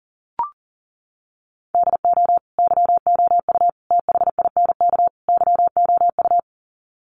Morse code, heard as a keyed tone, about 40 wpm, 700 Hz. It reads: DO YOU THINK YOU
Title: Sets of 4 Words 40wpm